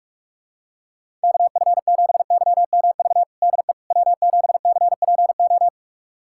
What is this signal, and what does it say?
Morse code, heard as a keyed tone, about 46 wpm, 700 Hz. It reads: KF7YMV DE W7CPO
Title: USA Callsign DE Callsign 45wpm